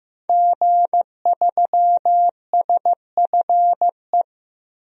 Morse code, heard as a keyed tone, about 15 wpm, 700 Hz. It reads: G3SFE
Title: DXCC-100 Call Signs - 5 Characters - Repeated 2x - 15wpm